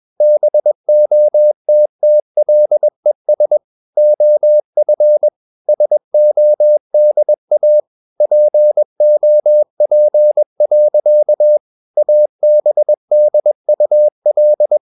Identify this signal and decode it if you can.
Morse code, heard as a keyed tone, about 21 wpm, 600 Hz. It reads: BOTTLES OF SODA POP. ABDUL
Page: CW med Gnister